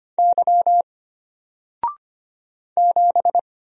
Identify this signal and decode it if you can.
Morse code, heard as a keyed tone, about 25 wpm, 700 Hz. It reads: Y 7
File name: Single-Letter-Number-25wpm